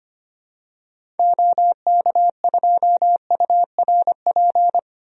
Morse code, heard as a keyed tone, about 25 wpm, 700 Hz. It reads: OK2URP
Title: DXCC-100 Call Signs - 6 Characters - 25wpm